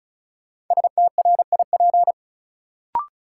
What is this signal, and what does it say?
Morse code, heard as a keyed tone, about 36 wpm, 700 Hz. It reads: STRIP
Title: Sentences from Top 1600 Words - New Word Drill Alt - 35wpm